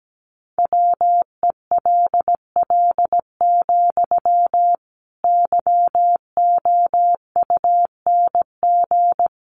Morse code, heard as a keyed tone, about 17 wpm, 700 Hz. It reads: WELL, YOUNG
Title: Sentences from Top 400 Words 17wpm